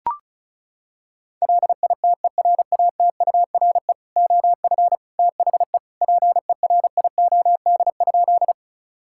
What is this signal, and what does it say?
Morse code, heard as a keyed tone, about 36 wpm, 700 Hz. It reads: LITERATURE OF THE PERIOD?
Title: Sentences from Top 1000 Words - Review Part 5 - 35wpm